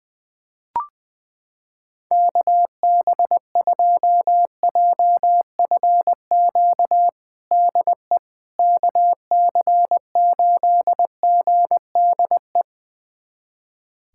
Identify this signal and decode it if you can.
Morse code, heard as a keyed tone, about 20 wpm, 700 Hz. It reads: KB2JFQ DE KC8GDE
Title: USA Callsign DE Callsign - 6 Characters - 20wpm